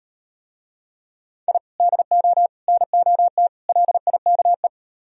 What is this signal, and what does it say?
Morse code, heard as a keyed tone, about 38 wpm, 700 Hz. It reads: I DO NOT LIKE